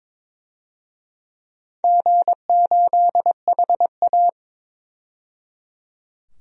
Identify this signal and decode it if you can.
Morse code, heard as a keyed tone, about 22 wpm, 700 Hz. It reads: G8HA
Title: DXCC-100 Call Signs - 4 Characters - 22wpm